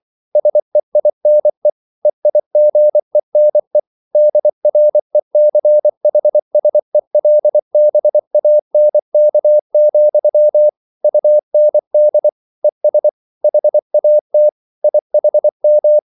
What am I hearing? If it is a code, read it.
Morse code, heard as a keyed tone, about 24 wpm, 600 Hz. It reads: SEINE EIGENE DRECHSELBANK, UND ES HAT IHM
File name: Liebespaar2 24wpm